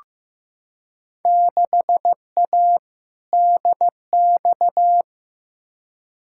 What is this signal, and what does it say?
Morse code, heard as a keyed tone, about 15 wpm, 700 Hz. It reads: 6A DX
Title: Field Day Exchanges Encore - 15wpm